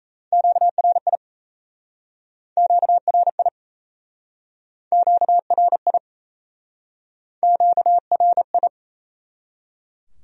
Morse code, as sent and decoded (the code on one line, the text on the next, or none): --.- .-. .. / --.- .-. .. / --.- .-. .. / --.- .-. ..
QRI QRI QRI QRI